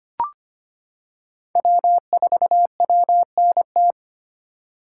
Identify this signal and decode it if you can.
Morse code, heard as a keyed tone, about 25 wpm, 700 Hz. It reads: W4WNT
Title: K1USN SST QSO Practice - 25wpm